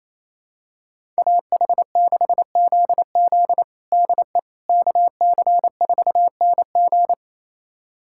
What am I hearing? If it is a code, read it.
Morse code, heard as a keyed tone, about 27 wpm, 700 Hz. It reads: AH6ZZ DE KC4NG
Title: USA Callsign DE Callsign - 5 Characters - 28wpm